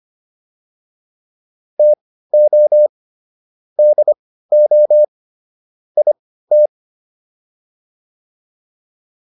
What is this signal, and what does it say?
Morse code, heard as a keyed tone, about 25 wpm, spaced as for 15 wpm, 600 Hz. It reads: TO DO IT